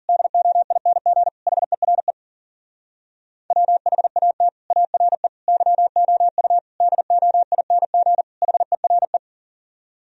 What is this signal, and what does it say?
Morse code, heard as a keyed tone, about 52 wpm, 700 Hz. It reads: DOING HERE WHAT ARE YOU DOING HERE